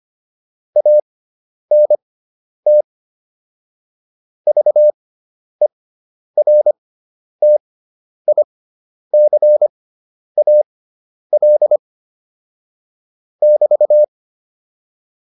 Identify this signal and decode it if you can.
Morse code, heard as a keyed tone, about 25 wpm, spaced as for 10 wpm, 600 Hz. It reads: ANT VERTICAL <BT>
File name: QSO207 10